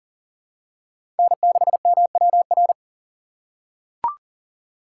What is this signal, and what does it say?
Morse code, heard as a keyed tone, about 40 wpm, 700 Hz. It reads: N6MWR
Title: USA Call Signs - 5 Characters - 2x Character Spacing - 40wpm